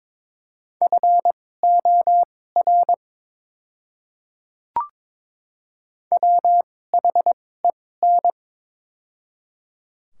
Morse code, heard as a keyed tone, about 22 wpm, spaced as for 16 wpm, 690 Hz. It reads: FOR WHEN
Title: 2x Character Spacing - Top 100 Words - 22wpm